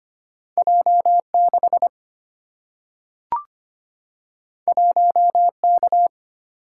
Morse code, heard as a keyed tone, about 25 wpm, 700 Hz. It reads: J6 1K
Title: Single-Letter Single-Number 25wpm